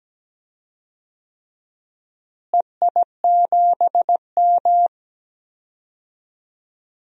Morse code, heard as a keyed tone, about 17 wpm, 700 Hz. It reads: EI7M
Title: DXCC-100 Call Signs - 4 Characters - Repeated 3x - 17wpm